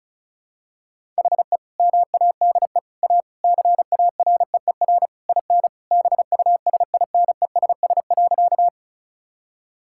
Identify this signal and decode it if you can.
Morse code, heard as a keyed tone, about 36 wpm, 700 Hz. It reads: HE MADE A CAREER IN BUSINESS.
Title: Sentences from Top 1000 Words - Review Part 6 - 35wpm